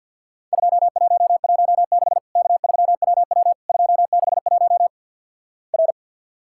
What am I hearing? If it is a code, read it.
Morse code, heard as a keyed tone, about 48 wpm, 700 Hz. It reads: 2116 K3WW 261 R
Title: Worked All Europe DX Contest - QTC Exchange with Repeats - 50wpm